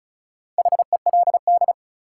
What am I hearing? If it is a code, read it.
Morse code, heard as a keyed tone, about 34 wpm, 700 Hz. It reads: HELD
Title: Sentences from Top 1600 Words - New Word Drill - 35wpm